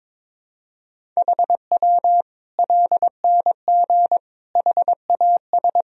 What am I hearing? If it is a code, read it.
Morse code, heard as a keyed tone, about 22 wpm, 700 Hz. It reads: HW LNG HAS
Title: Sentences from Top 100 Words and Abbreviations 22wpm